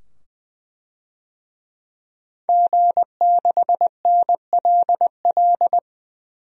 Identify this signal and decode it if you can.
Morse code, heard as a keyed tone, about 20 wpm, 700 Hz. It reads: G6NLL
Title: DXCC-100 Call Signs - 5 Characters - 20wpm